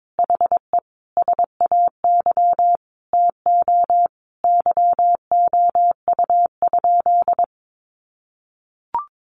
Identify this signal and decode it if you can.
Morse code, heard as a keyed tone, about 22 wpm, 700 Hz. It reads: HE SAY TO YOU?
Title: Sentences from Top 800 Words 22wpm